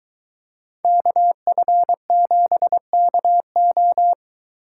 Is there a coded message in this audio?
Yes: Morse code from 0.8 to 4.1 s, heard as a keyed tone, about 23 wpm, 700 Hz. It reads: KF7KO